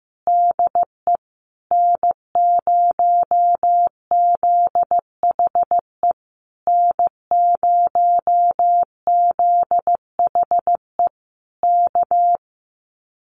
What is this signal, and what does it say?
Morse code, heard as a keyed tone, about 15 wpm, 700 Hz. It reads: DE N0ZHE N0ZHE K
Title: Common QSO Phrases Encore 15wpm